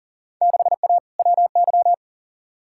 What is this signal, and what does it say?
Morse code, heard as a keyed tone, about 40 wpm, 700 Hz. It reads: 6A WY
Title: Field Day Exchanges Encore - 40wpm